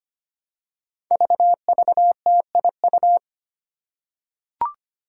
Morse code, heard as a keyed tone, about 25 wpm, 700 Hz. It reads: VVTIU
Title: Five Challenging Letter-Number Combinations 25wpm